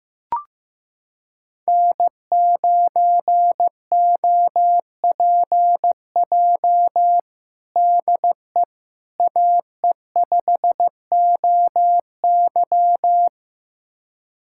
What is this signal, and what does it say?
Morse code, heard as a keyed tone, about 15 wpm, 700 Hz. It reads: N9OPJ DE AE5OY
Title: USA Callsign DE Callsign - 5 Characters - 15wpm